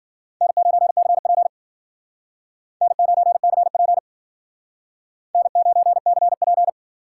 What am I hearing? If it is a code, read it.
Morse code, heard as a keyed tone, about 53 wpm, 700 Hz. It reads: N9CP N9CP N9CP